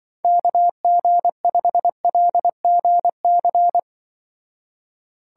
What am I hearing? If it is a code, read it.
Morse code, heard as a keyed tone, about 24 wpm, 700 Hz. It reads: KG5LGC
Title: Speed Racing - USA 2x3 Call Signs 20wpm